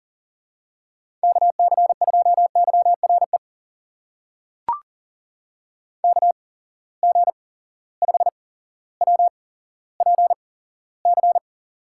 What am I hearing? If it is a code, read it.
Morse code, heard as a keyed tone, about 40 wpm, spaced as for 17 wpm, 700 Hz. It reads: KC2YRE KG5WPC
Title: USA Call Signs - 6 Characters - 8x Character Spacing - 40wpm